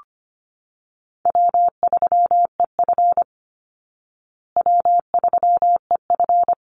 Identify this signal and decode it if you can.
Morse code, heard as a keyed tone, about 25 wpm, 700 Hz. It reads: W3EF W3EF
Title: Easy Call Signs Repeated 2x - 25wpm